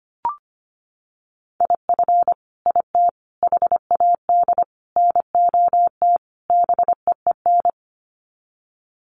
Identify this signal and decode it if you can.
Morse code, heard as a keyed tone, about 25 wpm, 700 Hz. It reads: IF IT HAD NOT BEEN
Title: Sets of 5 Words Encore 25wpm